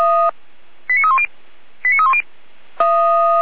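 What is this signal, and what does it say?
Five-tone selective call (ZVEI), heard as quick runs of five tones, tones of about 70 ms. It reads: 89310 89310
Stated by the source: Five-tone ZVEI selective call, code 89401, received by SDR.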